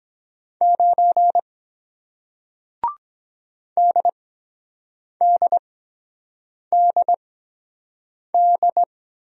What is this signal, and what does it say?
Morse code, heard as a keyed tone, about 23 wpm, spaced as for 7 wpm, 700 Hz. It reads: 9 DDDD